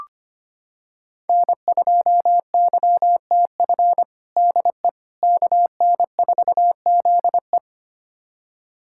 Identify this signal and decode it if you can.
Morse code, heard as a keyed tone, about 25 wpm, 700 Hz. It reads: N2YTF DE KN4ZE
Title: USA Callsign DE Callsign - 5 Characters - 25wpm